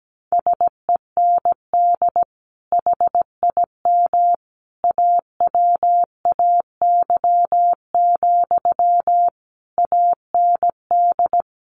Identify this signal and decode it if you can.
Morse code, heard as a keyed tone, about 17 wpm, 700 Hz. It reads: SEND HIM AWAY, AND
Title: Sentences from Top 1500 Words 17wpm